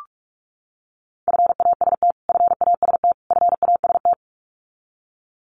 Morse code, heard as a keyed tone, about 44 wpm, 700 Hz. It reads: FAST FAST FAST
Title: Top 500 Words Repeated 45wpm